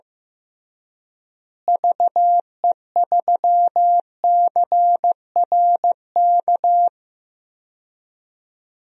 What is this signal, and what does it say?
Morse code, heard as a keyed tone, about 15 wpm, 700 Hz. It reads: VE3CRK
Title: DXCC-100 Call Signs - 6 Characters - Repeated 3x - 15wpm